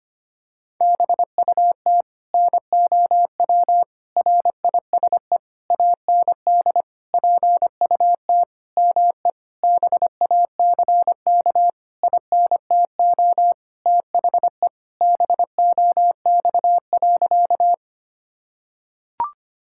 Morse code, heard as a keyed tone, about 25 wpm, 700 Hz. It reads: BUT NOW RISE AND PUT ME BACK INTO THE BOX.
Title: Sentences from Top 1000 Words - Review Part 6 - 25wpm